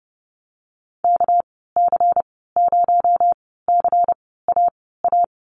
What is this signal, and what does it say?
Morse code, heard as a keyed tone, about 30 wpm, spaced as for 17 wpm, 700 Hz. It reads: KC0CAA
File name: Lesson 37g - Letter x - Call Signs 30wpm